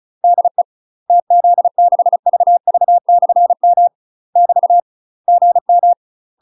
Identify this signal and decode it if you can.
Morse code, heard as a keyed tone, about 34 wpm, 700 Hz. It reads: DE TZ6VV/M <BT> GM
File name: UK MASTER QSOs TRAINING/35WPM